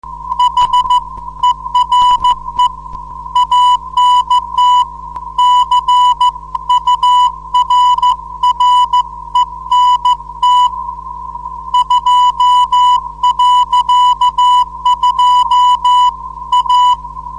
Morse code, heard as a keyed tone, about 14 wpm, 990 Hz. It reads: HERE ANT CURRENT 2.2 A
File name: 472khz qso web / 29march2014 / 07 rf-ant-current dl4yhf